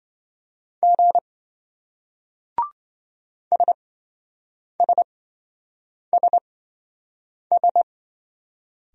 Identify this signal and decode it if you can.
Morse code, heard as a keyed tone, about 27 wpm, spaced as for 7 wpm, 700 Hz. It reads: G SSSS